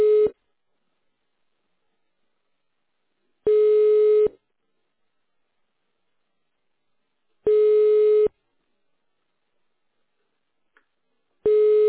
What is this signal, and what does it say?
Ringback tone (425 Hz, 1 s on, 3 s off).